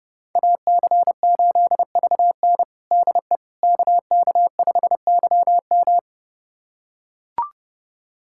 Morse code, heard as a keyed tone, about 32 wpm, 700 Hz. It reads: AC8VN DE KK5YM
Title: USA Callsign DE Callsign - 5 Characters - 30wpm